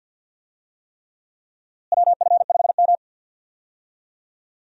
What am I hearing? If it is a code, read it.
Morse code, heard as a keyed tone, about 52 wpm, 700 Hz. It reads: WF5M
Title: USA Call Signs - 4 Characters - Repeated 3x - 50wpm